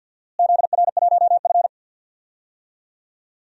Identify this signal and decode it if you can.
Morse code, heard as a keyed tone, about 48 wpm, 700 Hz. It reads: ZR1F
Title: DXCC-100 Call Signs - 4 Characters - Repeated 2x - 50wpm